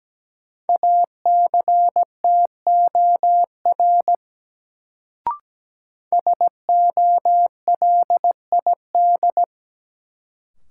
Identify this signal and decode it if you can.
Morse code, heard as a keyed tone, about 17 wpm, 700 Hz. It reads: ACTOR SOLID
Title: Sentences from Top 1500 Words - New Word Drill - 17wpm